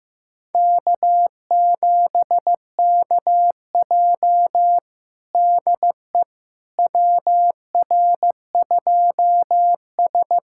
Morse code, heard as a keyed tone, about 15 wpm, 700 Hz. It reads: K7KJ DE WR2S
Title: USA Callsign DE Callsign - 4 Characters - 15wpm